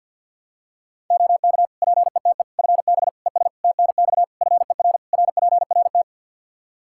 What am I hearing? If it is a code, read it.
Morse code, heard as a keyed tone, about 51 wpm, 700 Hz. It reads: OK PETE FB ES TNX FER RPRT
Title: Practice QSO - Exchange 2 - Info - 50wpm